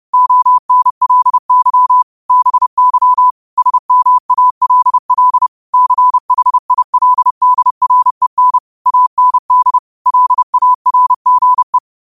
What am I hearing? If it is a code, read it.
Morse code, heard as a keyed tone, about 30 wpm, 1000 Hz. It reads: ONLY BY SMALL CHILDREN AND LARGE
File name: qotdfiles / 30wpm